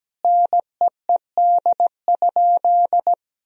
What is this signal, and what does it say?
Morse code, heard as a keyed tone, about 17 wpm, 700 Hz. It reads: NEED?
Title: Sentences from Top 1000 Words - Review Part 4 - 17wpm